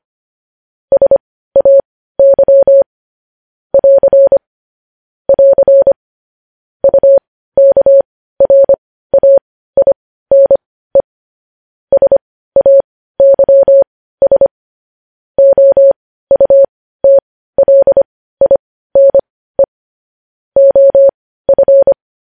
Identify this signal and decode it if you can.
Morse code, heard as a keyed tone, about 25 wpm, spaced as for 15 wpm, 550 Hz. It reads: SAY <AR> <AR> UKRAINE SAYS OUTLINE OF
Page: news headlines in morse code at 15 wpm